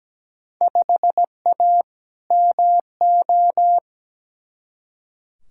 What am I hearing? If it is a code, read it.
Morse code, heard as a keyed tone, about 17 wpm, 700 Hz. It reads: 5A MO
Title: Field Day Exchanges Encore - 17wpm